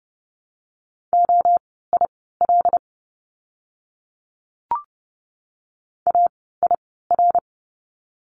Morse code, heard as a keyed tone, about 29 wpm, spaced as for 17 wpm, 700 Hz. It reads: OIL AIR
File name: Lesson 17f - Letter u - Words 30wpm